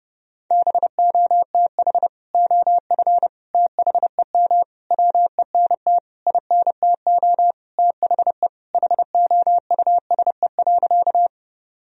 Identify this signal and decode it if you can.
Morse code, heard as a keyed tone, about 32 wpm, 700 Hz. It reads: BOTH OF THEM WENT INTO THE HOUSE.
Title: Sentences from Top 1000 Words - Review Part 1 - 30wpm